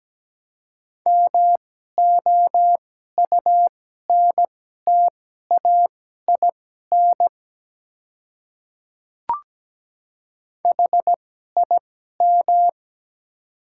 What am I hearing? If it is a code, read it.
Morse code, heard as a keyed tone, about 17 wpm, spaced as for 12 wpm, 700 Hz. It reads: MOUNTAIN HIM
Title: 2x Character Spacing - Top 300 Words - 17wpm